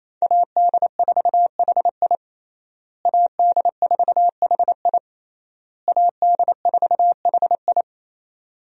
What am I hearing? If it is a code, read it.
Morse code, heard as a keyed tone, about 28 wpm, 700 Hz. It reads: AD4HI AD4HI AD4HI
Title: USA Call Signs - 5 Characters - Repeated 3x - 28wpm